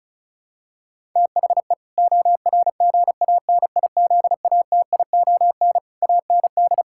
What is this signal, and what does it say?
Morse code, heard as a keyed tone, about 36 wpm, 700 Hz. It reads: THE ORGANIZATION AND
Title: Sentences from Top 1000 Words - Review Part 6 - 35wpm